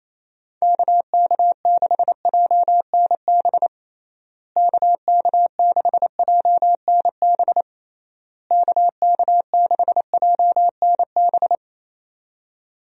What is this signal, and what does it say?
Morse code, heard as a keyed tone, about 28 wpm, 700 Hz. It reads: KK6JNB KK6JNB KK6JNB
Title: USA Call Signs - 6 Characters - Repeated 3x - 28wpm